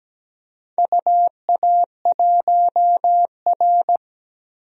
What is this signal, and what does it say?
Morse code, heard as a keyed tone, about 17 wpm, 700 Hz. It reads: UA1R
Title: DXCC-100 Call Signs - 4 Characters - 17wpm